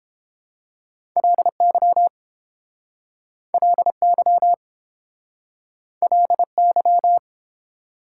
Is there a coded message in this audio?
Yes: Morse code from 1.2 to 7.2 s, heard as a keyed tone, about 30 wpm, 700 Hz. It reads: LY LY LY